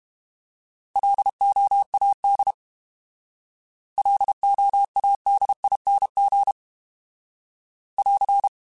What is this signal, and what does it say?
Morse code, heard as a keyed tone, about 32 wpm, 800 Hz. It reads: LOAD LOADING <AR>